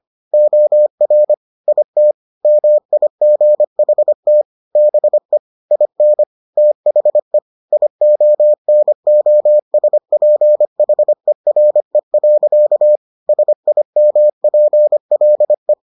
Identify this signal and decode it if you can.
Morse code, heard as a keyed tone, about 25 wpm, 600 Hz. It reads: OR IT MIGHT BE IN THE IONOSPHERE. SIMPLE